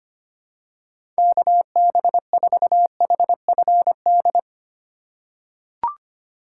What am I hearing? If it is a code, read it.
Morse code, heard as a keyed tone, about 25 wpm, 700 Hz. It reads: KB4HFD
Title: USA Call Signs - 6 Characters - 4x Character Spacing - 25wpm